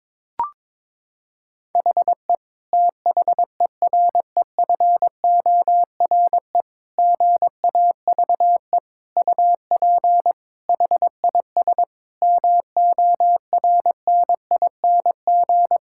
Morse code, heard as a keyed tone, about 22 wpm, 700 Hz. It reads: HE THEREFORE GAVE UP HIS MORNING
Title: Sentences from Top 1000 Words - Review Part 5 - 22wpm